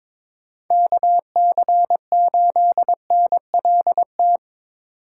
Morse code, heard as a keyed tone, about 22 wpm, 700 Hz. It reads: KC8NLT
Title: USA Call Signs - 6 Characters - 3x Character Spacing - 22wpm